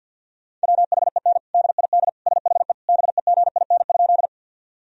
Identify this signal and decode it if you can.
Morse code, heard as a keyed tone, about 51 wpm, 700 Hz. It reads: WHEN DID SHE BEGIN?
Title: Sentences from Top 1000 Words - Review Part 1 - 50wpm